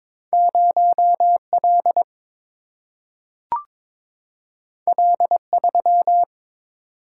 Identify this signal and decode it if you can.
Morse code, heard as a keyed tone, about 22 wpm, 700 Hz. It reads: 0 L L 3
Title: Single-Letter Single-Number 22wpm